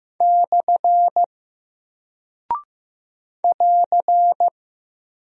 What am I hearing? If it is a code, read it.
Morse code, heard as a keyed tone, about 15 wpm, 700 Hz. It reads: / <AR>
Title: Single Character (Letter:Number:Punctuation:Prosign) Alt 15wpm